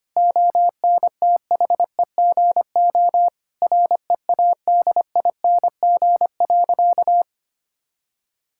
Morse code, heard as a keyed tone, about 25 wpm, 700 Hz. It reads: ONTHEGO READING.
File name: practice / 25wpm / Difficult Sentences with Hints - 25wpm